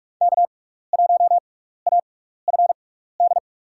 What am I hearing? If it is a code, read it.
Morse code, heard as a keyed tone, about 44 wpm, spaced as for 15 wpm, 690 Hz. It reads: K1AFD
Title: USA Call Signs - 5 Characters - 6x Character Spacing - 45wpm